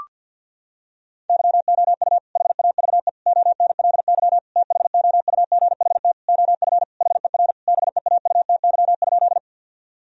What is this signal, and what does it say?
Morse code, heard as a keyed tone, about 50 wpm, 700 Hz. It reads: YOU HAVE ONLY THOUGHT OF HER BEAUTY?
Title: Sentences from Top 1000 Words - Review Part 2 - 50wpm